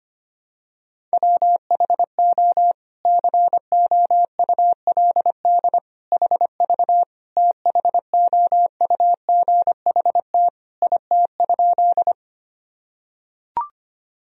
Morse code, heard as a keyed tone, about 25 wpm, 700 Hz. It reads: WHO COULD HV THOUGHT IT?
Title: Sentences from Top 300 Words and Abbreviations 25wpm